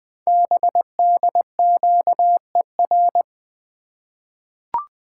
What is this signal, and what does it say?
Morse code, heard as a keyed tone, about 20 wpm, 700 Hz. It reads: BDQER
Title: Code Groups - 5 Letters - 20wpm